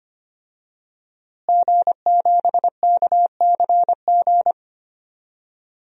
Morse code, heard as a keyed tone, about 25 wpm, 700 Hz. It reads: G7KCG
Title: DXCC-100 Call Signs - 5 Characters - Repeated 3x - 25wpm